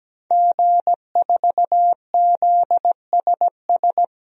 Morse code, heard as a keyed tone, about 17 wpm, 700 Hz. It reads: G4ZSS
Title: DXCC-100 Call Signs - 5 Characters - Repeated 2x - 17wpm